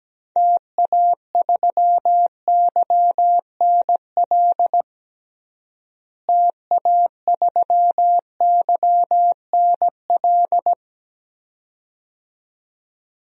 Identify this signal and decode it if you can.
Morse code, heard as a keyed tone, about 17 wpm, 700 Hz. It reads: TA3YNL TA3YNL
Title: DXCC-100 Call Signs - 6 Characters - Repeated 2x - 17wpm